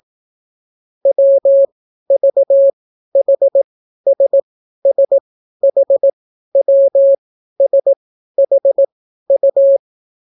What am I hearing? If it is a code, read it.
Morse code, heard as a keyed tone, about 18 wpm, spaced as for 12 wpm, 550 Hz. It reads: WVHSSHWSHU